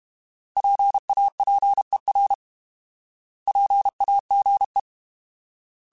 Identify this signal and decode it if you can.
Morse code, heard as a keyed tone, about 32 wpm, 800 Hz. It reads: PAPER PAGE